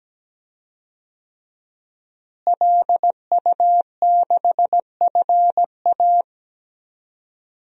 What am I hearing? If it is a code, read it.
Morse code, heard as a keyed tone, about 17 wpm, 700 Hz. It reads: LU6FA
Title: DXCC-100 Call Signs - 5 Characters - Repeated 2x - 17wpm